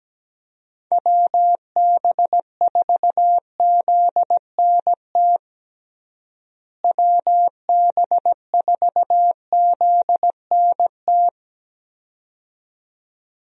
Morse code, heard as a keyed tone, about 17 wpm, 700 Hz. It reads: WB4ZNT WB4ZNT
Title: USA Call Signs - 6 Characters - Repeated 2x - 17wpm